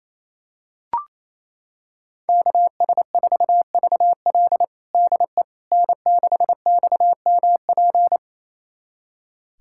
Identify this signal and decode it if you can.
Morse code, heard as a keyed tone, about 27 wpm, 700 Hz. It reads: KS4VL DE N6XMP
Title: USA Callsign DE Callsign - 5 Characters - 28wpm